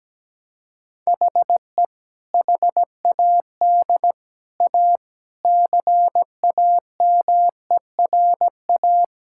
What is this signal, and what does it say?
Morse code, heard as a keyed tone, about 17 wpm, 700 Hz. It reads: HE HAD A CAMERA
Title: Sentences from Top 1000 Words - Review Part 5 - 17wpm